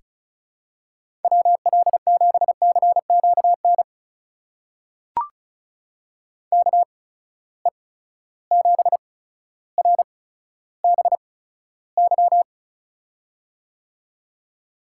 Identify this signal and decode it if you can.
Morse code, heard as a keyed tone, about 36 wpm, spaced as for 15 wpm, 700 Hz. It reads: WL7CQN KE7RBY
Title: USA Call Signs - 6 Characters - 8x Character Spacing - 35wpm